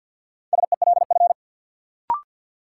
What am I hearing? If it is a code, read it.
Morse code, heard as a keyed tone, about 53 wpm, 700 Hz. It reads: SELF
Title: Sentences from Top 1000 Words - Review Part 1 - New Word Drill - 50wpm